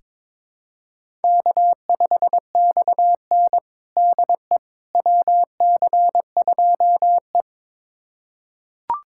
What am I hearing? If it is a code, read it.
Morse code, heard as a keyed tone, about 22 wpm, 700 Hz. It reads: K5XN DE WC2E
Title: USA Callsign DE Callsign - 4 Characters - 22wpm